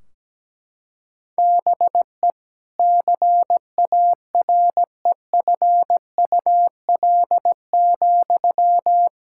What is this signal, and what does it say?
Morse code, heard as a keyed tone, about 17 wpm, 700 Hz. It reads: BE CAREFUL,
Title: Sentences from Top 1600 Words 17wpm